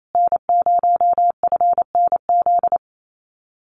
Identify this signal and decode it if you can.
Morse code, heard as a keyed tone, about 28 wpm, 700 Hz. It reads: N0FNZ
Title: Common QSO Phrases Encore 28wpm